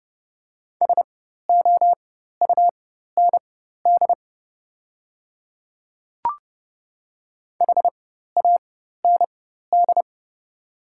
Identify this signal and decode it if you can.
Morse code, heard as a keyed tone, about 30 wpm, spaced as for 14 wpm, 700 Hz. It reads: SOUND HAND
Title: 4x Character Spacing - Top 300 Words - 30wpm